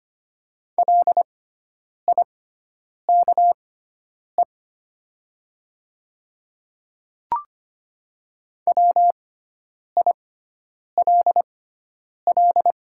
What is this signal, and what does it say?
Morse code, heard as a keyed tone, about 25 wpm, spaced as for 9 wpm, 700 Hz. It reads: LIKE WILL
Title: 6x Character Spacing - Top 100 Words - 25wpm